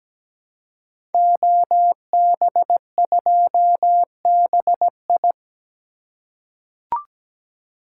Morse code, heard as a keyed tone, about 17 wpm, 700 Hz. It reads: OB2BI
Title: Code Groups - 5 Letter:Numbers - 17wpm